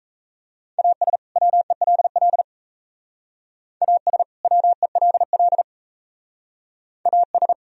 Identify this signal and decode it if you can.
Morse code, heard as a keyed tone, about 36 wpm, 700 Hz. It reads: AS WELL AS WELL AS